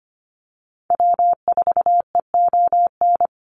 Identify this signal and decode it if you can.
Morse code, heard as a keyed tone, about 25 wpm, 700 Hz. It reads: W4EON
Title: NA SOTA Activator Callsigns 25wpm